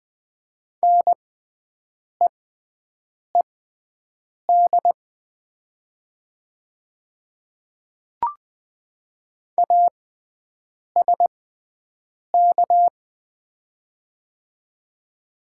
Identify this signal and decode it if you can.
Morse code, heard as a keyed tone, about 20 wpm, spaced as for 7 wpm, 700 Hz. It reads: NEED ASK
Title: 6x Character Spacing - Top 200 Words - 20wpm